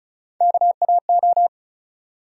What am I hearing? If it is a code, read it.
Morse code, heard as a keyed tone, about 36 wpm, 700 Hz. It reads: KAO
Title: USA Call Sign Postfixes - 3 Letters 35wpm